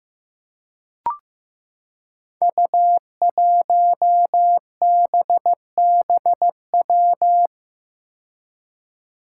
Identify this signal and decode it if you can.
Morse code, heard as a keyed tone, about 15 wpm, 700 Hz. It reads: U1BBW
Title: Five Challenging Letter-Number Combinations 15wpm